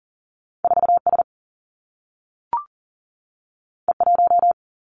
Morse code, heard as a keyed tone, about 42 wpm, 700 Hz. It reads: 4 S E 1
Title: Two Challenging Letter-Number Combinations 40wpm